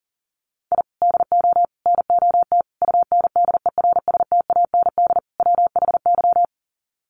Morse code, heard as a keyed tone, about 40 wpm, 700 Hz. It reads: I DO NOT UNDERSTAND WHY
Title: Sets of 5 Words Encore 40wpm